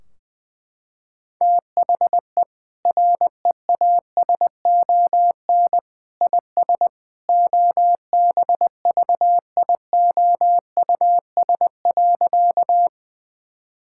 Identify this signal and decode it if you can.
Morse code, heard as a keyed tone, about 20 wpm, 700 Hz. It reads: THE REASON IS OBVIOUS.
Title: Sentences from Top 1600 Words 20wpm